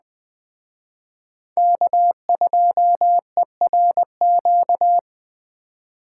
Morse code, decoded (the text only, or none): K2ERQ